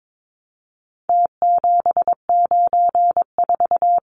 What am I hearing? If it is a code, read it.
Morse code, heard as a keyed tone, about 22 wpm, 700 Hz. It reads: T794
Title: Numbers - Contest Style - 4-Digit 22wpm